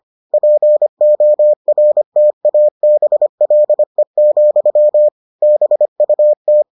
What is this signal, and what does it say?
Morse code, heard as a keyed tone, about 25 wpm, 600 Hz. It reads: PORTABLE, BUT